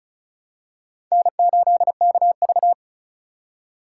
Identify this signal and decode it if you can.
Morse code, heard as a keyed tone, about 34 wpm, 700 Hz. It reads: N8KV
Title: USA Call Signs - 4x Character Spacing - 35wpm